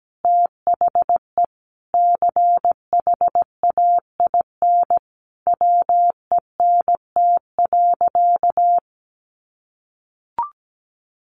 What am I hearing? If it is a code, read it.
Morse code, heard as a keyed tone, about 17 wpm, 700 Hz. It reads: THE CHAIN WENT.
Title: Sentences from Top 1500 Words 17wpm